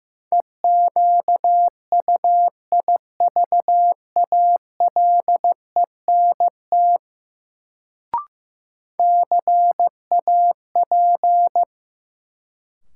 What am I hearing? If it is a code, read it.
Morse code, heard as a keyed tone, about 15 wpm, 700 Hz. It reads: EQUIVALENT CAP
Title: Sentences from Top 1600 Words - New Word Drill - 15wpm